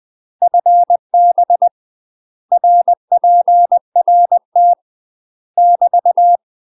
Morse code, heard as a keyed tone, about 20 wpm, 700 Hz. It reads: FB RPRT <BT>
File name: UK MASTER QSOs TRAINING / 20WPM / UK MASTER QSOs 0023